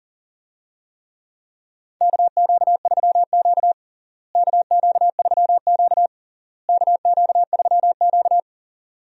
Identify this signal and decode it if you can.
Morse code, heard as a keyed tone, about 40 wpm, 700 Hz. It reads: KQ3Q KQ3Q KQ3Q
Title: USA Call Signs - 4 Characters - Repeated 3x - 40wpm